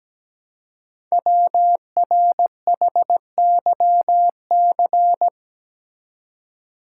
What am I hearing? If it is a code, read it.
Morse code, heard as a keyed tone, about 17 wpm, 700 Hz. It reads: WRHYC
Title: Code Groups - 5 Letters - 17wpm